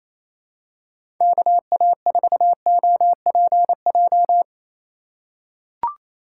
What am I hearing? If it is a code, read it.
Morse code, heard as a keyed tone, about 28 wpm, 700 Hz. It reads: KA4OPJ
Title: USA Call Signs - 6 Characters - 4x Character Spacing - 28wpm